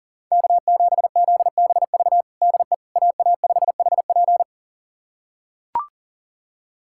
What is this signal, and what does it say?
Morse code, heard as a keyed tone, about 40 wpm, 690 Hz. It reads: K7ZBV DE AA5HP
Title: USA Callsign DE Callsign - 5 Characters - 40wpm